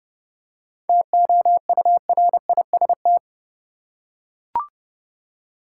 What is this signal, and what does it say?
Morse code, heard as a keyed tone, about 30 wpm, 700 Hz. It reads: TOURIST
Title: Seven Letter Words - 30wpm